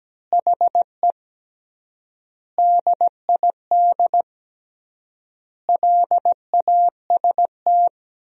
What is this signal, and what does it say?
Morse code, heard as a keyed tone, about 17 wpm, 700 Hz. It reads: HE DID LAST
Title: 3x Word-Spacing - Sentences from Top 300 Words - 17wpm